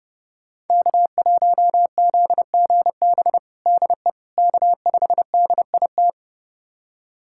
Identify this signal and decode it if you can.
Morse code, heard as a keyed tone, about 30 wpm, 700 Hz. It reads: K1ZGB DE K5DIT
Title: USA Callsign DE Callsign - 5 Characters - 30wpm